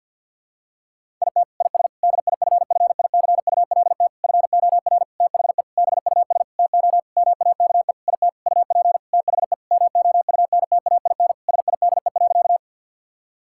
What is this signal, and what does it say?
Morse code, heard as a keyed tone, about 45 wpm, 700 Hz. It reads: IT IS DIFFICULT FOR THE BUS TO MAKE IT UP THE MOUNTAIN SIDE.
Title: Sentences from Top 1000 Words - Review Part 6 - 50wpm